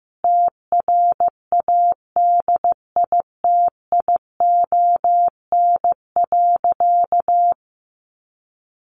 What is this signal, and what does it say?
Morse code, heard as a keyed tone, about 15 wpm, 700 Hz. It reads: TRADITION.
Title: Sentences from Top 1500 Words 15wpm